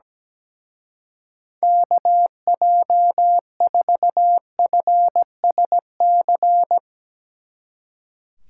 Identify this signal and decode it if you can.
Morse code, heard as a keyed tone, about 18 wpm, 700 Hz. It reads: KJ4FSC
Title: Speed Racing - USA 2x3 Call Signs 17wpm